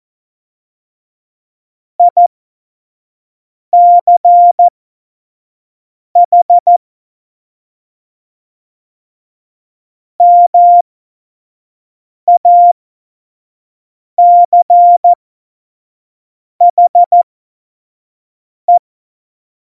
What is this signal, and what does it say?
Morse code, heard as a keyed tone, about 14 wpm, spaced as for 5 wpm, 700 Hz. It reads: ICH MACHE
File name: Morsekurs / Witze / Witz 04 August 2020, 25 F70